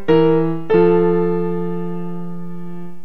Dial tone (North America), 350 + 440 Hz.